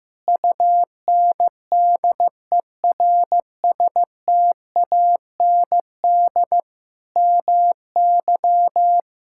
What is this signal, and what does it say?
Morse code, heard as a keyed tone, about 16 wpm, 700 Hz. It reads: UNDERSTAND MY
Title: Sentences from Top 1600 Words 15wpm